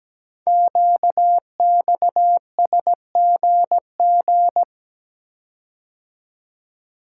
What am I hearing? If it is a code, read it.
Morse code, heard as a keyed tone, about 17 wpm, 700 Hz. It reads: QXSGG
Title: Code Groups - 5 Letter:Numbers - 17wpm